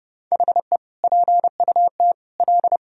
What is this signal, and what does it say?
Morse code, heard as a keyed tone, about 30 wpm, 700 Hz. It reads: HE PUT L
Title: Sentences from Top 1000 Words - Review Part 3 - 30wpm